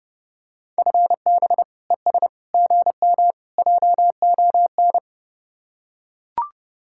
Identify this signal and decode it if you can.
Morse code, heard as a keyed tone, about 32 wpm, 700 Hz. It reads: FB ES GM JON